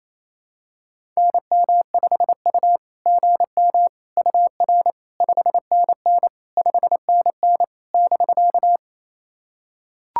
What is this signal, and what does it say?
Morse code, heard as a keyed tone, about 28 wpm, 690 Hz. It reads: NM5U GM UR 5NN 5NN <BK>
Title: SOTA QSO Practice 28wpm